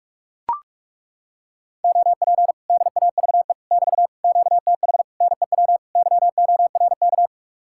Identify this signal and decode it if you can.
Morse code, heard as a keyed tone, about 40 wpm, 700 Hz. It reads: OP DAVE <BT> QTH NEW YORK